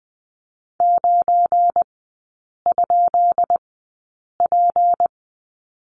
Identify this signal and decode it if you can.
Morse code, heard as a keyed tone, about 20 wpm, spaced as for 8 wpm, 700 Hz. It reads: 9?P
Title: 172 Lesson 28e - Number 9 - Three Characters 20wpm